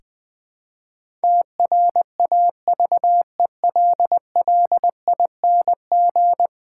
Morse code, heard as a keyed tone, about 20 wpm, 700 Hz. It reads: TRAVELLING
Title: Pangram Challenge v2 - 20wpm